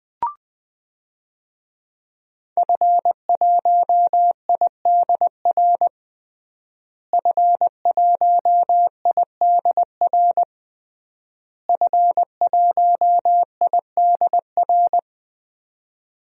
Morse code, heard as a keyed tone, about 21 wpm, 700 Hz. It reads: F1IDR F1IDR F1IDR
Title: DXCC-100 Call Signs - 5 Characters - Repeated 3x - 20wpm